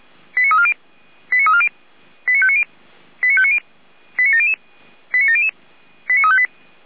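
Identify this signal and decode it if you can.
Five-tone selective call (ZVEI), heard as quick runs of five tones, tones of about 70 ms. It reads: 89340 89340 89590 89590 89700 89700 89358
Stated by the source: ZVEI five-tone selective call received on an SDR.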